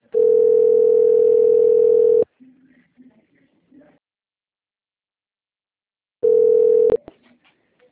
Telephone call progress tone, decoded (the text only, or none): ringback tone